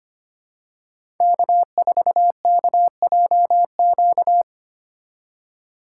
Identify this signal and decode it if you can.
Morse code, heard as a keyed tone, about 25 wpm, 700 Hz. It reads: K4KJQ
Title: USA Call Signs - 3x Character Spacing - 25wpm